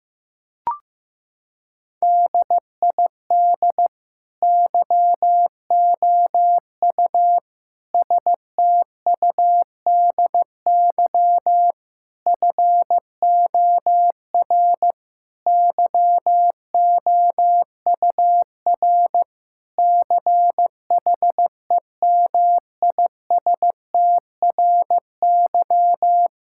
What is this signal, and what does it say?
Morse code, heard as a keyed tone, about 15 wpm, 700 Hz. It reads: DID YOU STUDY FOR YOUR CHEMISTRY
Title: Sentences from Top 1000 Words - Review Part 6 - 15wpm